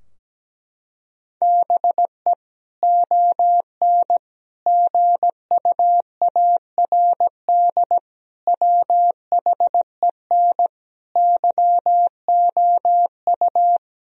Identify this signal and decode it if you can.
Morse code, heard as a keyed tone, about 17 wpm, 700 Hz. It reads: BE ON GUARD WHEN YOU
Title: Sentences from Top 1000 Words - Review Part 4 - 17wpm